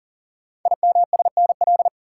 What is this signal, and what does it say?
Morse code, heard as a keyed tone, about 42 wpm, 690 Hz. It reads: IMSNL
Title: Code Groups - 5 Letters - 40wpm